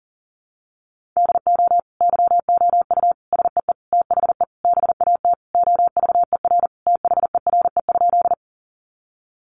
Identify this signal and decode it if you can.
Morse code, heard as a keyed tone, about 40 wpm, 700 Hz. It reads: DO YOU SEE THE BAT OVER THERE?
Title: Sentences from Top 1500 Words 40wpm